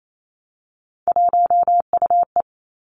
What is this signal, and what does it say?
Morse code, heard as a keyed tone, about 28 wpm, 700 Hz. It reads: 1UE
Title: Three Challenging Letter-Number Combinations 28wpm